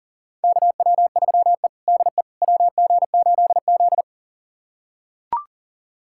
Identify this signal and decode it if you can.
Morse code, heard as a keyed tone, about 40 wpm, 700 Hz. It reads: KW3E DE WG8Z
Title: USA Callsign DE Callsign - 4 Characters - 40wpm